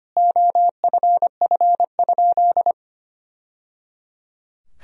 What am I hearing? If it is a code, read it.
Morse code, heard as a keyed tone, about 25 wpm, 700 Hz. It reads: OFF?
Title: Sentences from Top 1000 Words - Review Part 3 - 25wpm